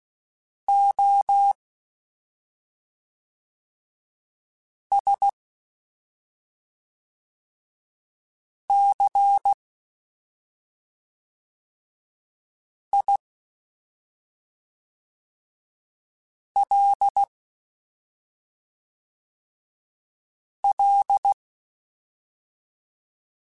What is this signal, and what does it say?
Morse code, heard as a keyed tone, about 16 wpm, 800 Hz. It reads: O S C I L L
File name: Special INGs / Special INGs-mp3s / SlowToFaster ing @5